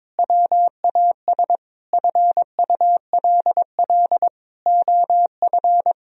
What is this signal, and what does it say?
Morse code, heard as a keyed tone, about 21 wpm, 700 Hz. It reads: WAS FULL OF
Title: Sentences from Top 1000 Words - Review Part 4 - 22wpm